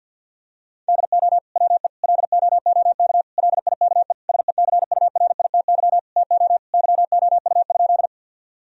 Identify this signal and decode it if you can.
Morse code, heard as a keyed tone, about 50 wpm, 700 Hz. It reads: DO WE LOOK LIKE SECURITY TO YOU?
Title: Sentences from Top 1000 Words - Review Part 4 - 50wpm